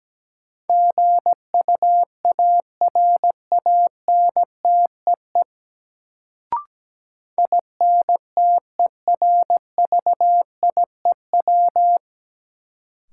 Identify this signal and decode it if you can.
Morse code, heard as a keyed tone, about 17 wpm, 700 Hz. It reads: GUARANTEE INTERVIEW
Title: Sentences from Top 1600 Words - New Word Drill - 17wpm